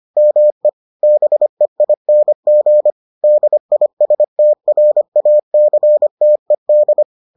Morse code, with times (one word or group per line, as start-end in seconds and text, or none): 0.2-0.7 ME
1.0-2.9 BEING
3.2-7.0 DISTRACTED